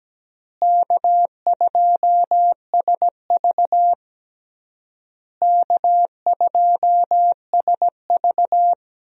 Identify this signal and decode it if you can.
Morse code, heard as a keyed tone, about 17 wpm, 700 Hz. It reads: K2SV K2SV
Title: USA Call Signs - 4 Characters - Repeated 3x - 17wpm